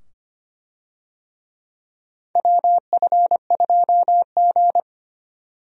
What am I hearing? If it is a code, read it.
Morse code, heard as a keyed tone, about 25 wpm, 700 Hz. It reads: WF2G
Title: USA Call Signs - 4 Characters - 25wpm